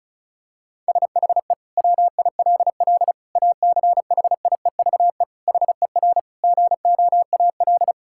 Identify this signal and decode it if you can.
Morse code, heard as a keyed tone, about 36 wpm, 700 Hz. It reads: SHE WILL ACHIEVE HER GOAL
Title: Sentences from Top 1000 Words - Review Part 6 - 35wpm